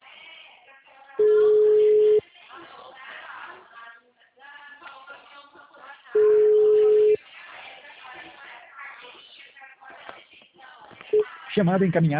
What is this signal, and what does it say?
Ringback tone (Europe, 425 Hz), with speech.